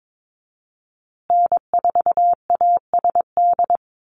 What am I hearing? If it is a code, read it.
Morse code, heard as a keyed tone, about 22 wpm, 700 Hz. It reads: N4ASD
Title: USA Call Signs Repeated 2x 22wpm